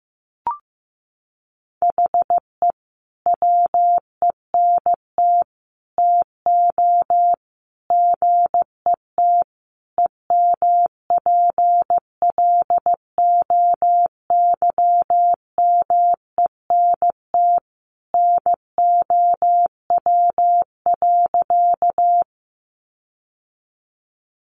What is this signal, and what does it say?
Morse code, heard as a keyed tone, about 15 wpm, 700 Hz. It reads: HE WENT TO GET EMPLOYMENT NOW.
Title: Sentences from Top 1500 Words 15wpm